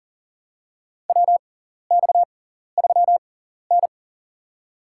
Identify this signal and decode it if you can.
Morse code, heard as a keyed tone, about 40 wpm, spaced as for 14 wpm, 700 Hz. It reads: WX3N
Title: USA Call Signs - 4 Characters - 6x Character Spacing - 40wpm